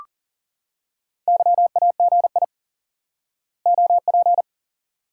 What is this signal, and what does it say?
Morse code, heard as a keyed tone, about 40 wpm, 700 Hz. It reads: YAGI OP
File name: IQE Course 3 - Lesson 89 - YAGI - Two Elements - 40wpm